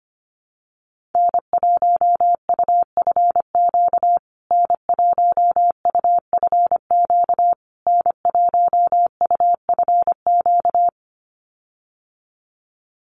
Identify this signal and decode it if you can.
Morse code, heard as a keyed tone, about 25 wpm, 700 Hz. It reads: N1UFQ N1UFQ N1UFQ
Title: USA Call Signs Repeated 3x 25wpm